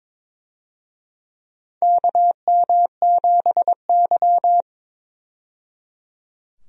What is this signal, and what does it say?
Morse code, heard as a keyed tone, about 22 wpm, 700 Hz. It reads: KM7Y
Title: USA Call Signs - 4 Characters - 22wpm